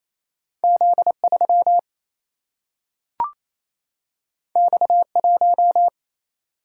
Morse code, heard as a keyed tone, about 28 wpm, 700 Hz. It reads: Z 3 X 1
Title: Single-Letter Single-Number 28wpm